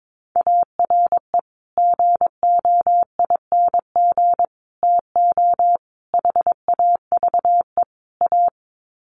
Morse code, heard as a keyed tone, about 22 wpm, 700 Hz. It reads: ARE GOING TO HAVE A
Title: Sets of 5 Words Encore2 22wpm